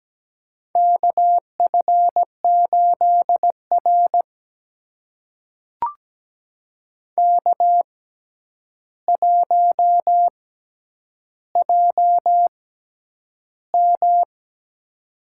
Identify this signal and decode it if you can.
Morse code, heard as a keyed tone, about 17 wpm, spaced as for 9 wpm, 700 Hz. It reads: KF8R K1JM
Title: USA Call Signs - 6x Character Spacing - 17wpm